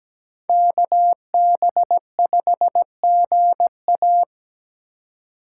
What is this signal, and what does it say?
Morse code, heard as a keyed tone, about 17 wpm, 700 Hz. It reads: KB5GA
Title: K1USN SST QSO Practice - 17wpm